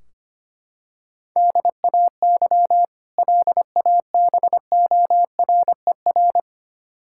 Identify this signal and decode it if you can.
Morse code, heard as a keyed tone, about 25 wpm, 700 Hz. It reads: DAY LABORER
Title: K1USN SST QSO Practice 25wpm